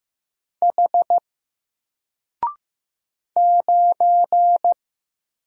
Morse code, heard as a keyed tone, about 15 wpm, 700 Hz. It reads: H 9
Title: Single Letter-Number 15wpm